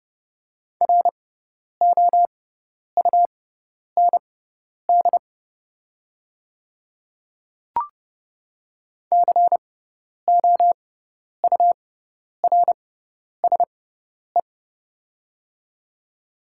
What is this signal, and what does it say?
Morse code, heard as a keyed tone, about 30 wpm, spaced as for 10 wpm, 690 Hz. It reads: ROUND COURSE
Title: 6x Character Spacing - Top 500 Words - 30wpm